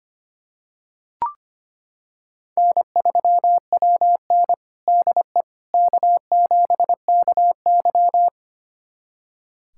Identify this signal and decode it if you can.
Morse code, heard as a keyed tone, about 25 wpm, 700 Hz. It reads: N3WN DE K7KY
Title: USA Callsign DE Callsign - 4 Characters - 25wpm